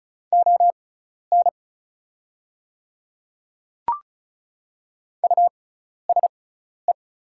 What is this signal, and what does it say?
Morse code, heard as a keyed tone, about 34 wpm, spaced as for 12 wpm, 700 Hz. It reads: ON USE
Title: 6x Character Spacing - Top 200 Words - 35wpm